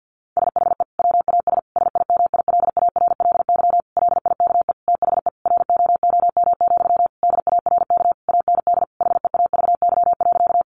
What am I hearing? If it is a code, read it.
Morse code, heard as a keyed tone, about 50 wpm, 700 Hz. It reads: SHE WAS SIMILARLY LIKE THE ROOM, DARK AND HEAVY.
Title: Sentences from Top 1400 Words 50wpm